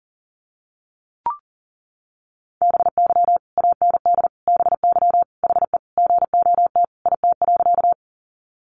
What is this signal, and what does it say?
Morse code, heard as a keyed tone, about 40 wpm, 700 Hz. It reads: BY AND BY HE GOT IT.
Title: Sentences from Top 300 Words 40wpm